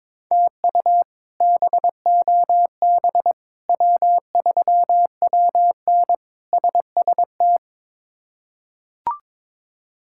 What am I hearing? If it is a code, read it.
Morse code, heard as a keyed tone, about 23 wpm, 700 Hz. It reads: TU BOB W3WN SST
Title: K1USN SST QSO Practice - 22wpm